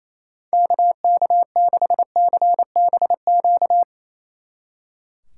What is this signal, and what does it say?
Morse code, heard as a keyed tone, about 35 wpm, 700 Hz. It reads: KK6CBQ
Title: Speed Racing - USA 2x3 Call Signs 28wpm